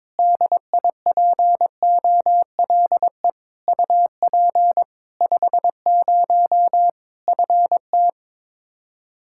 Morse code, heard as a keyed tone, about 22 wpm, 700 Hz. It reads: DIPOLE UP 50 FT
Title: Practice QSO - Exchange 2 - Info - 22wpm